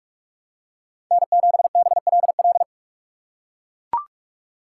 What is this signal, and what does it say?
Morse code, heard as a keyed tone, about 43 wpm, 700 Hz. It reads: N7BLL
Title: USA Call Signs - 5 Characters - 3x Character Spacing - 45wpm